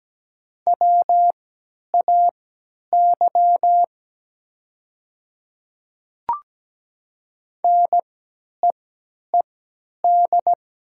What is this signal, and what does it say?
Morse code, heard as a keyed tone, about 17 wpm, spaced as for 10 wpm, 700 Hz. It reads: WAY NEED
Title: 3x Character Spacing - Top 200 Words - 17wpm